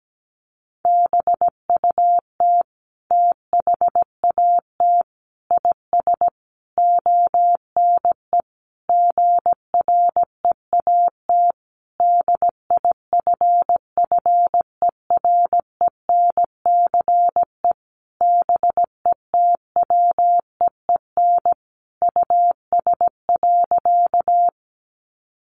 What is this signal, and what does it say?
Morse code, heard as a keyed tone, about 17 wpm, 700 Hz. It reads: BUT THAT IS ONE GREAT DIFFERENCE BETWEEN US.
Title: Sentences from Top 800 Words 17wpm